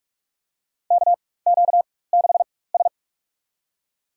Morse code, heard as a keyed tone, about 43 wpm, spaced as for 21 wpm, 700 Hz. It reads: KQ6S
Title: USA Call Signs - 4 Characters - 4x Character Spacing - 45wpm